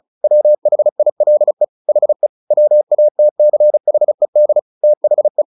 Morse code, heard as a keyed tone, about 34 wpm, 600 Hz. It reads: WHILE HE WATCHED THE
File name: the old man and the sea 2 35wpm